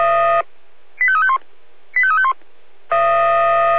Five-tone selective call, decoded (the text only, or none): 85351 85351